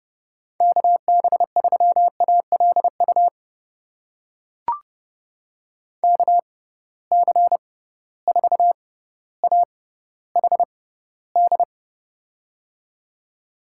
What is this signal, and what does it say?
Morse code, heard as a keyed tone, about 30 wpm, spaced as for 15 wpm, 700 Hz. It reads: KB3ALU KC4AHD
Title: USA Call Signs - 6x Character Spacing - 30wpm